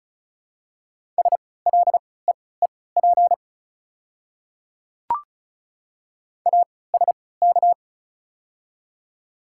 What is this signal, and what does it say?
Morse code, heard as a keyed tone, about 36 wpm, spaced as for 20 wpm, 700 Hz. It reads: SLEEP ASK
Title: 3x Character Spacing - Top 500 Words - 35wpm